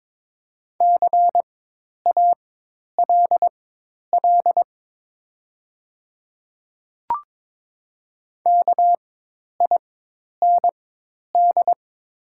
Morse code, heard as a keyed tone, about 22 wpm, spaced as for 10 wpm, 700 Hz. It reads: CALL KIND
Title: 4x Character Spacing - Top 500 Words - 22wpm